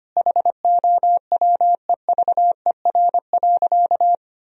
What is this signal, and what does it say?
Morse code, heard as a keyed tone, about 25 wpm, 700 Hz. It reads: HOWEVER.
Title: Sentences from Top 1000 Words - Review Part 4 - 25wpm